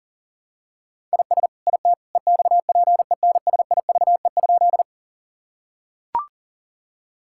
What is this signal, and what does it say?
Morse code, heard as a keyed tone, about 37 wpm, 700 Hz. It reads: IS IT EXPENSIVE?
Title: Sentences from Top 1000 Words - Review Part 6 - 40wpm